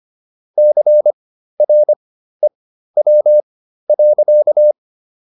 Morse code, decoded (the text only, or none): CREW.